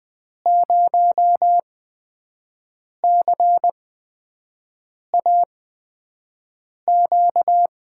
Morse code, heard as a keyed tone, about 20 wpm, spaced as for 5 wpm, 700 Hz. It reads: 0CAQ
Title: USA Call Signs - 6 Characters - 8x Character Spacing - 20wpm